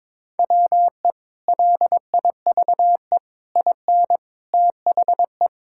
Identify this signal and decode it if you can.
Morse code, heard as a keyed tone, about 21 wpm, 700 Hz. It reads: WE LIVE IN THE
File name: practice / 22wpm / Sentences with 3-Word Phrases - 22wpm